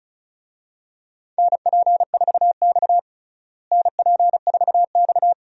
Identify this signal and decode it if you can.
Morse code, heard as a keyed tone, about 34 wpm, 700 Hz. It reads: NP4X NP4X
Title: USA Call Signs - 4 Characters - Repeated 3x - 35wpm